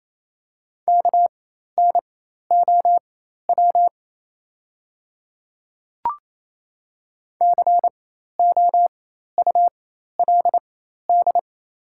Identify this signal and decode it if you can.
Morse code, heard as a keyed tone, about 28 wpm, spaced as for 13 wpm, 700 Hz. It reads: KNOW COULD
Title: 4x Character Spacing - Top 100 Words - 28wpm